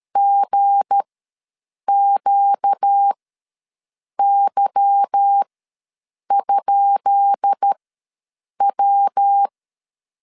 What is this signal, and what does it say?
Morse code, heard as a keyed tone, about 13 wpm, spaced as for 7 wpm, 800 Hz. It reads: GQY?W
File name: LEZIONI / 10 SIGNS / 10B INT ws 12wpm 3527